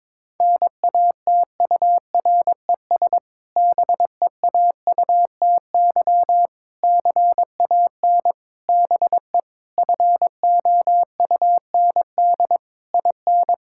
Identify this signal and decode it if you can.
Morse code, heard as a keyed tone, about 22 wpm, 700 Hz. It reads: NATURES BEAUTY CAN BE FOUND IN
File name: Difficult Sentences with Hints - 22wpm